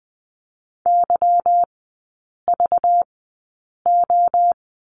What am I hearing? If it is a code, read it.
Morse code, heard as a keyed tone, about 20 wpm, spaced as for 8 wpm, 700 Hz. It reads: YVO
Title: 198 Lesson 32e - Letter k - Three Characters 20wpm